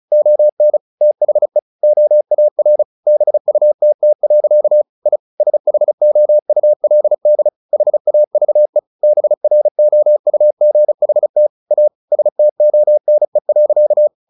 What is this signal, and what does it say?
Morse code, heard as a keyed tone, about 34 wpm, 600 Hz. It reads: ON THE OAR BUTT. I SHOULD HAVE BROUGHT A STONE.
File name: the old man and the sea 2 35wpm